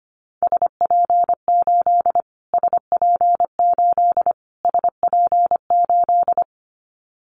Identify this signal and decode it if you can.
Morse code, heard as a keyed tone, about 25 wpm, 700 Hz. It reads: SP8 SP8 SP8
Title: Callsign Prefixes for Top 25 DXCC - Repeated 3x - 25wpm